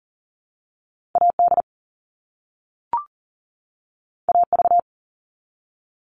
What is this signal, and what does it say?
Morse code, heard as a keyed tone, about 38 wpm, 700 Hz. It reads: AD AV
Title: All Cut Numbers - 2-Digit 40wpm